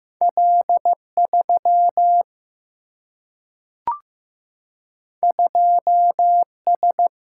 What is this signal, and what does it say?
Morse code, heard as a keyed tone, about 15 wpm, 700 Hz. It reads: L 3 2 S
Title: Single-Letter Single-Number 15wpm